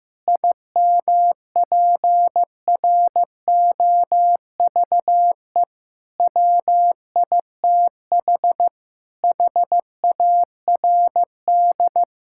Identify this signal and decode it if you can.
Morse code, heard as a keyed tone, about 15 wpm, 700 Hz. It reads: IMPROVE WITH HARD
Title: Sentences from Top 1000 Words - Review Part 5 - 15wpm